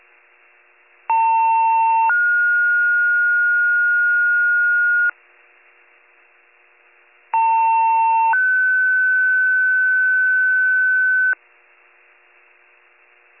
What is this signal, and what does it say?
Two-tone page, heard as a one-second tone, then a three-second tone, about 900 then 1430 Hz, about 900 then 1500 Hz. It is heard from 1.1 to 11.3 s.